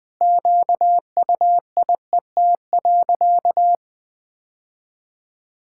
Morse code, heard as a keyed tone, about 21 wpm, 700 Hz. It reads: QUIET.
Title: Sentences with 5-Word Phrases - 20wpm